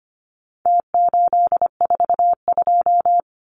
Morse code, heard as a keyed tone, about 25 wpm, 700 Hz. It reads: T842
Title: Numbers - Contest Style - 4-Digit 25wpm